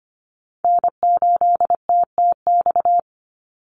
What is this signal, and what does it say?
Morse code, heard as a keyed tone, about 25 wpm, 700 Hz. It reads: N8TTX
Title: USA Call Signs 25wpm